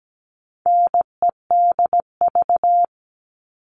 Morse code, heard as a keyed tone, about 17 wpm, 700 Hz. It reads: NEDV
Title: All Cut Numbers - 4-Digit 17wpm